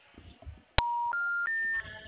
Special information tone.
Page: Cordeiropolis 2